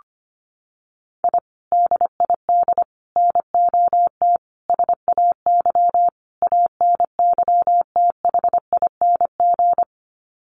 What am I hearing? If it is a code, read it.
Morse code, heard as a keyed tone, about 25 wpm, 700 Hz. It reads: I DID NOT SAY ANYTHING
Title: Sets of 5 Words Encore2 25wpm